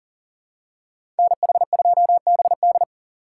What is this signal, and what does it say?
Morse code, heard as a keyed tone, about 40 wpm, 700 Hz. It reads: NH2BD
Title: USA Call Signs - 5 Characters - 2x Character Spacing - 40wpm